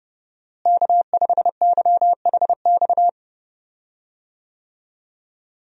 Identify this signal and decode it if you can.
Morse code, heard as a keyed tone, about 30 wpm, 700 Hz. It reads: K5YHX
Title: USA Call Signs - 5 Characters - Repeated 3x - 30wpm